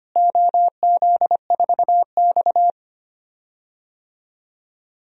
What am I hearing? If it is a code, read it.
Morse code, heard as a keyed tone, about 25 wpm, 700 Hz. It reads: OZ4X
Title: DXCC-100 Call Signs - 4 Characters - Repeated 3x - 25wpm